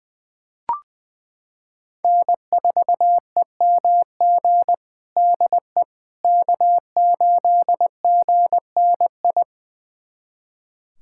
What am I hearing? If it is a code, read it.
Morse code, heard as a keyed tone, about 20 wpm, 700 Hz. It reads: N4EMG DE K8GNI
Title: USA Callsign DE Callsign - 5 Characters - 20wpm